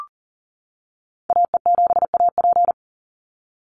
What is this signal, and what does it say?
Morse code, heard as a keyed tone, about 41 wpm, 700 Hz. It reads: AE7AP
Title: NA SOTA Activator Callsigns 40wpm